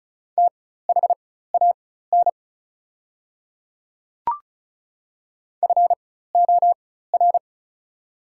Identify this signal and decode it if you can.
Morse code, heard as a keyed tone, about 36 wpm, spaced as for 16 wpm, 700 Hz. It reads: THAN FOR
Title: 4x Character Spacing - Top 100 Words - 35wpm